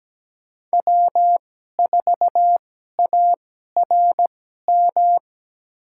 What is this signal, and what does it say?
Morse code, heard as a keyed tone, about 17 wpm, spaced as for 12 wpm, 700 Hz. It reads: W4ARM
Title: USA Call Signs - 5 Characters - 2x Character Spacing - 17wpm